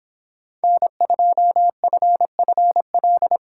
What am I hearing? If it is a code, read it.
Morse code, heard as a keyed tone, about 23 wpm, 700 Hz. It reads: N2FFL